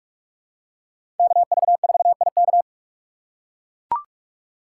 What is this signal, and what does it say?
Morse code, heard as a keyed tone, about 41 wpm, 700 Hz. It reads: KV4IK
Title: USA Call Signs - 5 Characters - 2x Character Spacing - 45wpm